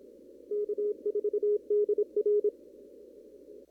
Morse code, heard as a keyed tone, about 26 wpm, 420 Hz. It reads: K4DR
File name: CW / kt5j wpx cw-op-k5pi